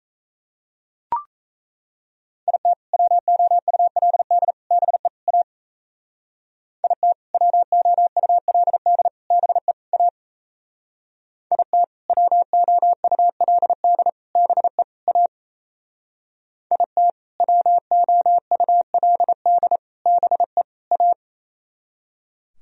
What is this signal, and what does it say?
Morse code, heard as a keyed tone, about 36 wpm, 700 Hz. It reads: IT WOULD BE A IT WOULD BE A IT WOULD BE A IT WOULD BE A